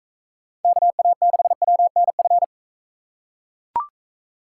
Morse code, heard as a keyed tone, about 42 wpm, 700 Hz. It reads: KA6WNF